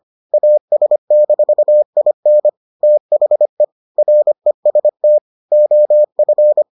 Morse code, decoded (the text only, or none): AS-IN THE REST OF